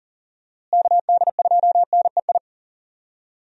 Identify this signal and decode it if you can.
Morse code, heard as a keyed tone, about 40 wpm, 700 Hz. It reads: KD2NEI
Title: USA Call Signs - 6 Characters - Repeated 2x - 40wpm